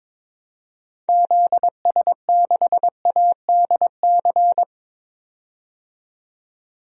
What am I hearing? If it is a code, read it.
Morse code, heard as a keyed tone, about 21 wpm, 700 Hz. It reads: ZS6ADC
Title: DXCC-100 Call Signs - 6 Characters - 22wpm